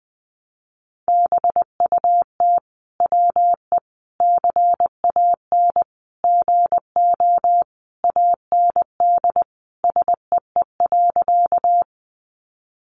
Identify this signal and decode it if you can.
Morse code, heard as a keyed tone, about 20 wpm, 700 Hz. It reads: BUT WE CAN GO AND SEE.
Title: Sentences from Top 100 Words 20wpm